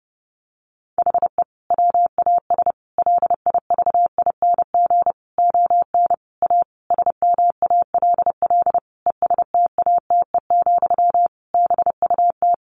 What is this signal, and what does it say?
Morse code, heard as a keyed tone, about 30 wpm, 700 Hz. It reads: HE WAS LIVING ON A SMALL ESTATE, BUT
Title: Sentences from Top 1200 Words 30wpm